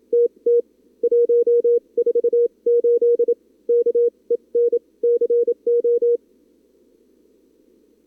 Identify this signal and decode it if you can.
Morse code, heard as a keyed tone, about 27 wpm, spaced as for 24 wpm, 440 Hz. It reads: TT 148 K EN CO
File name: CWSprint / Feb2022 / k5tr-feb-na-sprint-cw-k5ot -0100